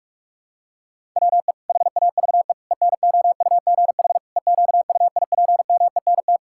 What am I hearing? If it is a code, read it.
Morse code, heard as a keyed tone, about 44 wpm, 700 Hz. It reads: WE HAVE ENOUGH EQUIPMENT
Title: Sentences from Top 1000 Words - Review Part 5 - 45wpm